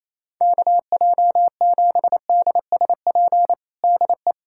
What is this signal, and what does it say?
Morse code, heard as a keyed tone, about 28 wpm, 700 Hz. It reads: KJ7DSP DE
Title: USA Callsign DE Callsign - 6 Characters - 28wpm